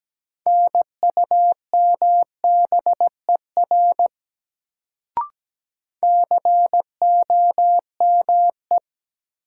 Morse code, heard as a keyed tone, about 17 wpm, 700 Hz. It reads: NUMBER COME
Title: Sentences from Top 100 Words - New Word Drill - 17wpm